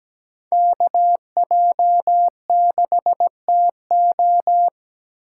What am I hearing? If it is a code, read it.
Morse code, heard as a keyed tone, about 17 wpm, 700 Hz. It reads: KJ6TO
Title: USA Call Signs - 5 Characters - 6x Character Spacing - 17wpm